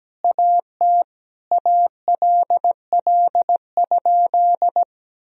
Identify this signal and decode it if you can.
Morse code, heard as a keyed tone, about 17 wpm, 700 Hz. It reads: AT ALL?
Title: Sentences from Top 1000 Words - Review Part 4 - 17wpm